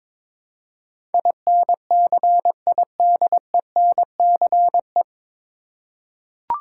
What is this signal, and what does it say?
Morse code, heard as a keyed tone, about 21 wpm, 700 Hz. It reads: INCIDENCE
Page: Morse Code Ninja practice 22wpm